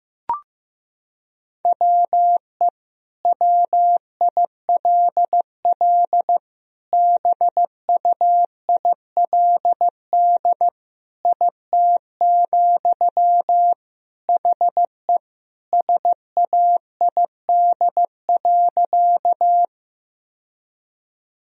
Morse code, heard as a keyed tone, about 15 wpm, 700 Hz. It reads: WE WILL BUILD IT, HE SAID.
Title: Sentences from Top 1000 Words - Review Part 1 - 15wpm